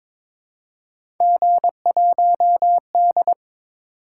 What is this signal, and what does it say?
Morse code, heard as a keyed tone, about 22 wpm, 700 Hz. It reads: G1D
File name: practice / 22wpm / Three Letter-Number Combinations - 22wpm